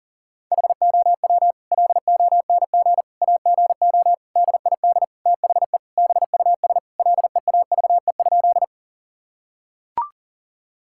Morse code, heard as a keyed tone, about 40 wpm, 700 Hz. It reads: HOW LONG AGO DID THE BUS LEAVE?
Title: Sentences from Top 1000 Words - Review Part 6 - 40wpm